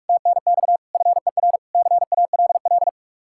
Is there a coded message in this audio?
Yes: Morse code from 0.1 to 2.9 s, heard as a keyed tone, about 58 wpm, 700 Hz. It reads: TNX FER CALL